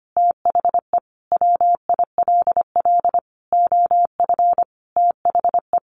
Morse code, heard as a keyed tone, about 25 wpm, 700 Hz. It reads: THE WILL OF THE
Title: Sentences from Top 1500 Words 25wpm